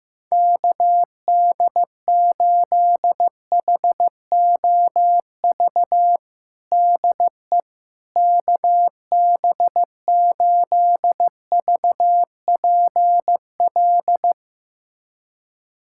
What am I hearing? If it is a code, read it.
Morse code, heard as a keyed tone, about 15 wpm, 700 Hz. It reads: KD8HOV DE KB8VPL
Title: USA Callsign DE Callsign - 6 Characters - 15wpm